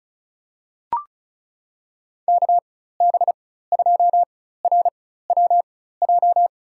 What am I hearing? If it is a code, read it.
Morse code, heard as a keyed tone, about 34 wpm, spaced as for 16 wpm, 700 Hz. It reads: KB2RWJ
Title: USA Call Signs - 6 Characters - 4x Character Spacing - 35wpm